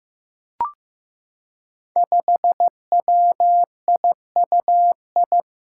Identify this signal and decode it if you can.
Morse code, heard as a keyed tone, about 15 wpm, 700 Hz. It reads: 5WIUI
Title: Five Challenging Letter-Number Combinations 15wpm